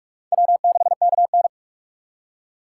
Morse code, heard as a keyed tone, about 45 wpm, 700 Hz. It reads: W6KN
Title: USA Call Signs - 4 Characters - 4x Character Spacing - 45wpm